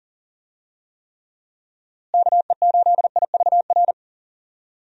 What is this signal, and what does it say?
Morse code, heard as a keyed tone, about 40 wpm, 700 Hz. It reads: KE8IVR
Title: USA Call Signs - 6 Characters - Repeated 2x - 40wpm